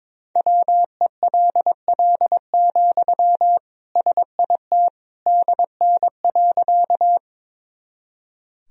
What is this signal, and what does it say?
Morse code, heard as a keyed tone, about 22 wpm, 700 Hz. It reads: WELL, SIT DN.
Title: Sentences from Top 500 Words and Abbreviations 22wpm